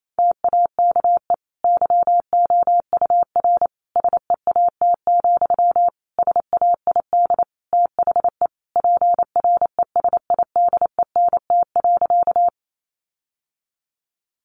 Morse code, heard as a keyed tone, about 28 wpm, 700 Hz. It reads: TAKE YOUR SEAT, SAID THE PRESIDENT.
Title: Sentences from Top 1500 Words 28wpm